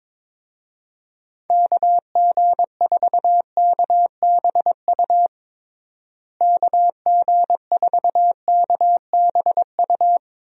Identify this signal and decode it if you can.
Morse code, heard as a keyed tone, about 22 wpm, 700 Hz. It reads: KG4KBU KG4KBU
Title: USA Call Signs - 6 Characters - Repeated 2x - 22wpm